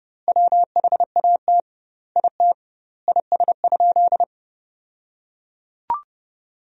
Morse code, heard as a keyed tone, about 30 wpm, 700 Hz. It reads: WHAT IT IS?
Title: 2x Word-Spacing - Sentences from Top 100 Words - 30wpm